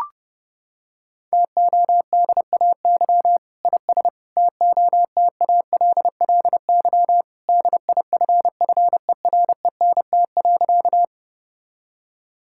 Morse code, heard as a keyed tone, about 30 wpm, 700 Hz. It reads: TODAY IS TOTALLY DIFFERENT.
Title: Sentences from Top 1600 Words 30wpm